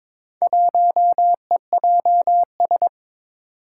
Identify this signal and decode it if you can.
Morse code, heard as a keyed tone, about 22 wpm, 700 Hz. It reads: 1EJS
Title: Four Challenging Letter-Number Combinations 22wpm